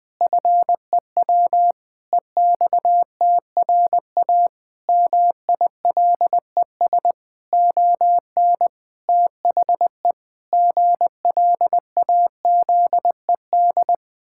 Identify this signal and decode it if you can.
Morse code, heard as a keyed tone, about 21 wpm, 700 Hz. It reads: FEW EXTRA MILES ON THE GLAZED
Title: Pangram Challenge v2 - 20wpm